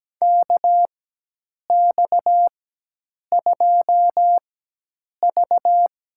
Morse code, heard as a keyed tone, about 17 wpm, spaced as for 8 wpm, 700 Hz. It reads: KX2V
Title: USA Call Signs - 4 Characters - 4x Character Spacing - 17wpm